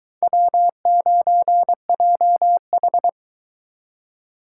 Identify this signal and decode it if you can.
Morse code, heard as a keyed tone, about 22 wpm, 700 Hz. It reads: W9JH